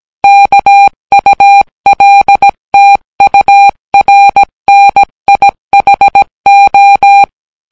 Morse code, heard as a keyed tone, about 18 wpm, 800 Hz. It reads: KULTURNIHO